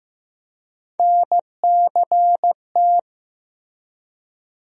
Morse code, heard as a keyed tone, about 15 wpm, 700 Hz. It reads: NCT
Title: USA Call Sign Postfixes - 3 Letters 15wpm